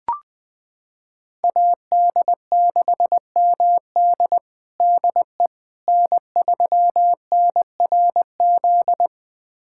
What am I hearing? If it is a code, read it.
Morse code, heard as a keyed tone, about 20 wpm, 700 Hz. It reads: AD6MD DE N3NRZ
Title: USA Callsign DE Callsign - 5 Characters - 20wpm